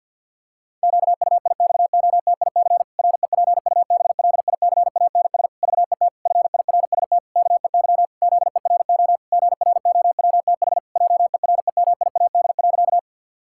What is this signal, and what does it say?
Morse code, heard as a keyed tone, about 50 wpm, 700 Hz. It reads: QUIXOTIC REPUBLICANS VET FIRST KEY ZERO GROWTH JEREMIAD.
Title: Pangram Challenge v2 - 50wpm